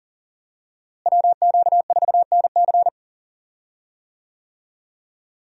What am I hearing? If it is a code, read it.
Morse code, heard as a keyed tone, about 40 wpm, 700 Hz. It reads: WQ4NC
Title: USA Call Signs - 5 Characters - Repeated 3x - 40wpm